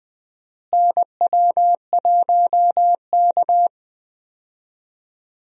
Morse code, heard as a keyed tone, about 20 wpm, 700 Hz. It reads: NW1K
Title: USA Call Signs - 4 Characters - Repeated 3x - 20wpm